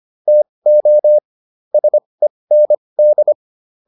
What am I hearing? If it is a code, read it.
Morse code, heard as a keyed tone, about 25 wpm, spaced as for 20 wpm, 600 Hz. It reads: TO SEND